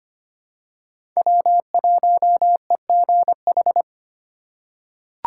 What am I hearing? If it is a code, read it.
Morse code, heard as a keyed tone, about 25 wpm, 700 Hz. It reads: W1EGH
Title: USA Call Signs - 2x Character Spacing - 25wpm